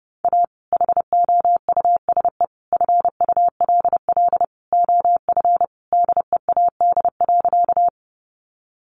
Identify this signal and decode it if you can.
Morse code, heard as a keyed tone, about 30 wpm, 700 Hz. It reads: A HOUSE FULL OF DEAD.
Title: Sentences from Top 1500 Words 30wpm